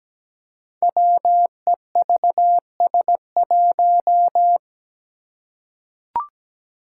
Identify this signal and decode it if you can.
Morse code, heard as a keyed tone, about 17 wpm, 700 Hz. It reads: WEVS1
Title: Five Challenging Letter-Number Combinations 17wpm